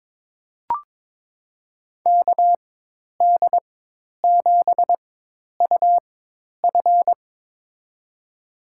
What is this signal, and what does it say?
Morse code, heard as a keyed tone, about 22 wpm, spaced as for 10 wpm, 700 Hz. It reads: KD7UF
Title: USA Call Signs - 5 Characters - 4x Character Spacing - 22wpm